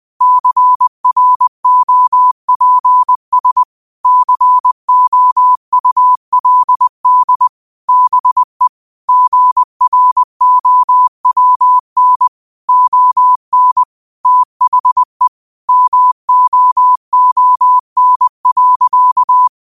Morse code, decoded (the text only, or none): CROPS COULD BE GROWN ON THE MOON.